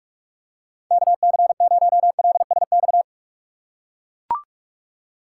Morse code, heard as a keyed tone, about 44 wpm, 700 Hz. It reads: KC0LSX
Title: USA Call Signs - 6 Characters - 6x Character Spacing - 45wpm